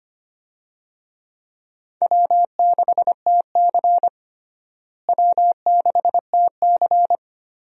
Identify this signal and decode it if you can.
Morse code, heard as a keyed tone, about 25 wpm, 700 Hz. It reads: W6TC W6TC
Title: USA Call Signs - 4 Characters - Repeated 2x - 25wpm